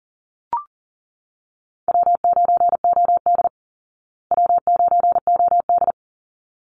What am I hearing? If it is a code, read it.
Morse code, heard as a keyed tone, about 40 wpm, 700 Hz. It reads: W9OD W9OD
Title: Call Signs Repeated 3x - 40wpm